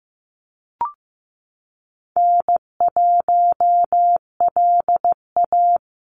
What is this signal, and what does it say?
Morse code, heard as a keyed tone, about 15 wpm, 700 Hz. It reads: N1LA
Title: USA Call Signs 15wpm